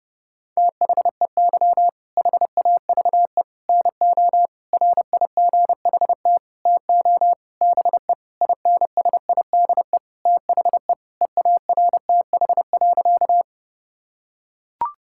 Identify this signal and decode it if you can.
Morse code, heard as a keyed tone, about 30 wpm, 700 Hz. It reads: THEY HAVE NO RIGHT TO BE INSIDE THE EARTH.
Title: Sentences from Top 1000 Words - Review Part 6 - 30wpm